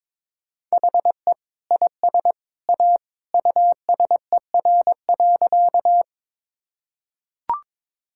Morse code, heard as a keyed tone, about 22 wpm, 700 Hz. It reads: HE IS A USER.
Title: Sentences from Top 1000 Words - Review Part 5 - 22wpm